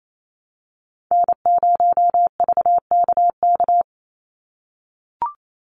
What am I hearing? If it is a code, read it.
Morse code, heard as a keyed tone, about 28 wpm, 700 Hz. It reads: N0VKK
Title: USA Call Signs 28wpm